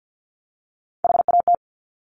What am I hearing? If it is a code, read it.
Morse code, heard as a keyed tone, about 50 wpm, 700 Hz. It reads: HAT
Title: Top Three Letter Combinations in Words 50wpm